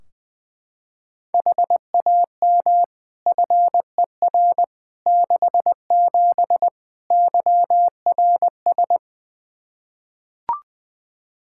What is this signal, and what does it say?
Morse code, heard as a keyed tone, about 21 wpm, 700 Hz. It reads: HAM FER 67 YRS
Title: Practice QSO - Exchange 3 - More Info - 20wpm